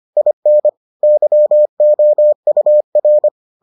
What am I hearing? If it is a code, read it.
Morse code, heard as a keyed tone, about 25 wpm, 600 Hz. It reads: IN YOUR